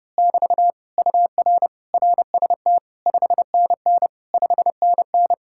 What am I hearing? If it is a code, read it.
Morse code, heard as a keyed tone, about 28 wpm, 700 Hz. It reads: <BT> UR RST 5NN 5NN
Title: Speed-Racing Common QSO Phrases Encore - 30wpm